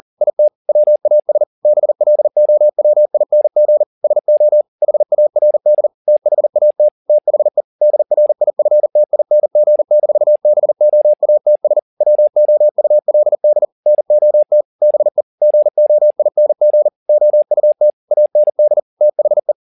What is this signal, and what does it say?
Morse code, heard as a keyed tone, about 40 wpm, 600 Hz. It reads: IT WAS BLOWING SO HARD THAT THE DRIFTING-BOATS WOULD NOT BE GOING OUT AND THE
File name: the old man and the sea 2 40wpm